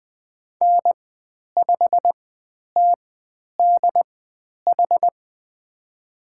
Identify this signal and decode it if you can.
Morse code, heard as a keyed tone, about 20 wpm, spaced as for 10 wpm, 700 Hz. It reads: N5TDH
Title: 095 Lesson 16g - Letter c - Call Signs 20wpm